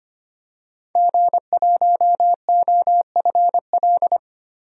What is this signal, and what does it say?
Morse code, heard as a keyed tone, about 25 wpm, 700 Hz. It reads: G1OFL
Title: DXCC-100 Call Signs - 5 Characters - 25wpm